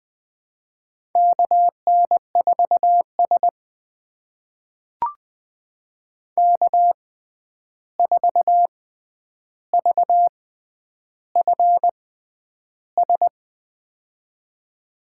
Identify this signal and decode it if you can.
Morse code, heard as a keyed tone, about 20 wpm, spaced as for 10 wpm, 700 Hz. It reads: KN4S K4VFS
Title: USA Call Signs - 6x Character Spacing - 20wpm